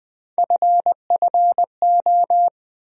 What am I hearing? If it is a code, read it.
Morse code, heard as a keyed tone, about 20 wpm, 700 Hz. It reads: FFO
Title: USA Call Sign Postfixes - 3 Letters 20wpm